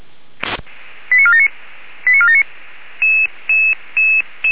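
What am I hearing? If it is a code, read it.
Five-tone selective call (ZVEI), heard as quick runs of five tones, tones of about 70 ms. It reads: 89479 89479
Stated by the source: ZVEI five-tone call 85385 received on an SDR.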